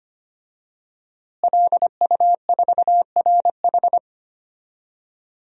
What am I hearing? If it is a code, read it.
Morse code, heard as a keyed tone, about 25 wpm, 700 Hz. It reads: LU4RH
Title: DXCC-100 Call Signs - 5 Characters - Repeated 2x - 25wpm